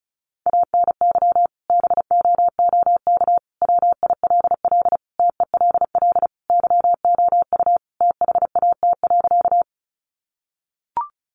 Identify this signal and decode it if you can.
Morse code, heard as a keyed tone, about 34 wpm, 700 Hz. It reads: ANY BOOK WILL TELL YOU THAT.
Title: Sentences from Top 300 Words 35wpm